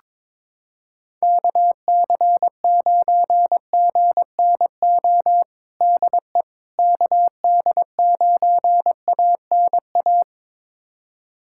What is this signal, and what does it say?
Morse code, heard as a keyed tone, about 22 wpm, 700 Hz. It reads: KC9GNO DE KD9ANA
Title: USA Callsign DE Callsign - 6 Characters - 22wpm